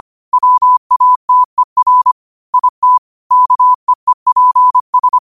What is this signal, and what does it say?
Morse code, heard as a keyed tone, about 25 wpm, 1000 Hz. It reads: WATER IT KEEPS